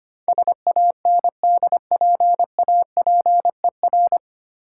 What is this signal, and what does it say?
Morse code, heard as a keyed tone, about 25 wpm, 700 Hz. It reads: SANDPAPER
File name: practice / 25wpm / Difficult Words with Hints - 25wpm